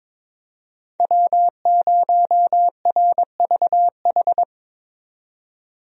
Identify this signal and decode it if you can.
Morse code, heard as a keyed tone, about 22 wpm, 700 Hz. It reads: W0RVH
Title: USA Call Signs - 5 Characters - 22wpm